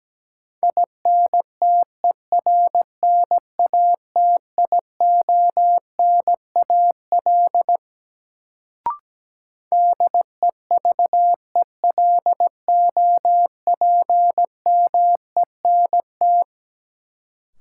Morse code, heard as a keyed tone, about 17 wpm, 700 Hz. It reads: INTERNATIONAL DEVELOPMENT
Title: Sentences from Top 1000 Words - Review Part 4 - New Word Drill - 17wpm